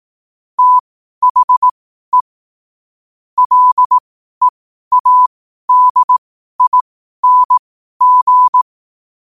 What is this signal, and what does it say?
Morse code, heard as a keyed tone, about 18 wpm, spaced as for 12 wpm, 1000 Hz. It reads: THE LEADING
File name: qotdfiles / 12wpm